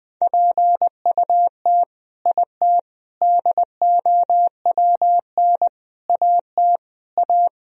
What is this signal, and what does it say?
Morse code, heard as a keyed tone, about 20 wpm, 700 Hz. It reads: PUT IT DOWN AT A
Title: Sentences from Top 1000 Words - Review Part 3 - 20wpm